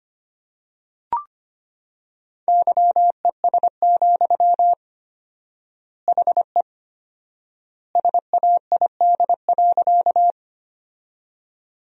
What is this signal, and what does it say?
Morse code, heard as a keyed tone, about 25 wpm, 700 Hz. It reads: YES, HE SAID.
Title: 4x Word-Spacing - Sentences from Top 500 Words - 25wpm